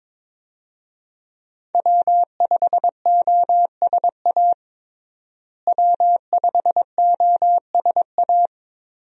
Morse code, heard as a keyed tone, about 22 wpm, 700 Hz. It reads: W5OSA W5OSA
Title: USA Call Signs - 5 Characters - Repeated 2x - 22wpm